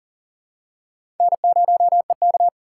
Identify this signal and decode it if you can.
Morse code, heard as a keyed tone, about 39 wpm, 700 Hz. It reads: N0EK
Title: USA Call Signs - 2x Character Spacing - 40wpm